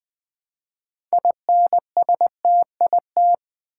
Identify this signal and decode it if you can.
Morse code, heard as a keyed tone, about 20 wpm, 700 Hz. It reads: INSTIT
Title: Build A Word - 3 Letters At A Time - 9-Letter Words - 20wpm